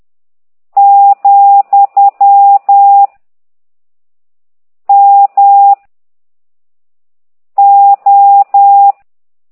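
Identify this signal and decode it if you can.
Morse code, heard as a keyed tone, about 10 wpm, 800 Hz. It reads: , M O
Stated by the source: Morse code lesson 5 exercise 10 by CT2IDW, containing only characters M, O, G, Z, Q and comma.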